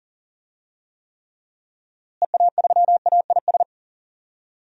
Morse code, heard as a keyed tone, about 40 wpm, 700 Hz. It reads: EA3AIS
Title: DXCC-100 Call Signs - 6 Characters - Repeated 2x - 40wpm